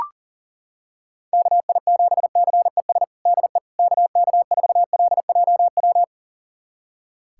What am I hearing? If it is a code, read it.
Morse code, heard as a keyed tone, about 40 wpm, 700 Hz. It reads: KI7CES DE KK4LJW
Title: USA Callsign DE Callsign 40wpm